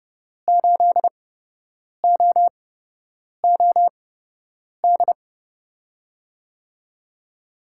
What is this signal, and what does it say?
Morse code, heard as a keyed tone, about 30 wpm, spaced as for 8 wpm, 700 Hz. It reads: 8OOD
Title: USA Call Signs - 5 Characters - 8x Character Spacing - 30wpm